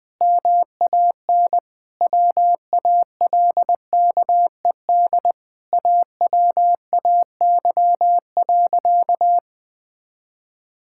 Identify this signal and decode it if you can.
Morse code, heard as a keyed tone, about 21 wpm, 700 Hz. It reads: MAN WALKED AWAY.
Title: Sentences with 2-Word Phrases - Spoken Beginning - 20wpm